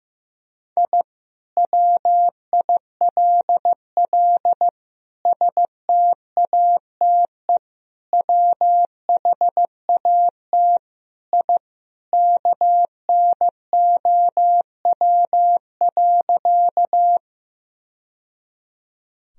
Morse code, heard as a keyed tone, about 15 wpm, 700 Hz. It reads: I WILL STATE WHAT I KNOW.
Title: Sentences from Top 1000 Words - Review Part 2 - 15wpm